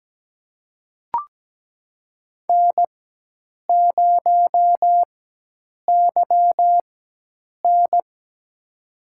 Morse code, heard as a keyed tone, about 17 wpm, spaced as for 8 wpm, 700 Hz. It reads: N0YN
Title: USA Call Signs - 4 Characters - 4x Character Spacing - 17wpm